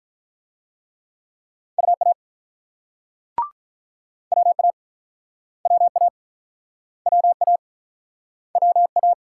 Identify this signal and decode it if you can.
Morse code, heard as a keyed tone, about 50 wpm, 700 Hz. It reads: UA WA WA WA WA